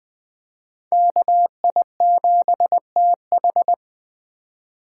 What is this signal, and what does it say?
Morse code, heard as a keyed tone, about 20 wpm, 700 Hz. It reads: KI7TH
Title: Speed Racing - USA Call Signs 17wpm